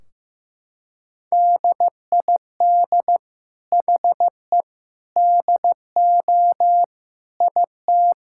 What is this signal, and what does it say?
Morse code, heard as a keyed tone, about 16 wpm, 700 Hz. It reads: DID HE DO IT
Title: Sentences from Top 1600 Words 15wpm